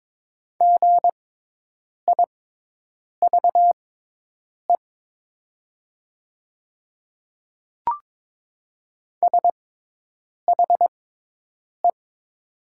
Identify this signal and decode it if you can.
Morse code, heard as a keyed tone, about 22 wpm, spaced as for 8 wpm, 700 Hz. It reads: GIVE SHE
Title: 6x Character Spacing - Top 300 Words - 22wpm